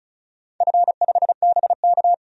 Morse code, heard as a keyed tone, about 36 wpm, 700 Hz. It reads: F5BK
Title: DXCC-100 Call Signs - 4 Characters - 35wpm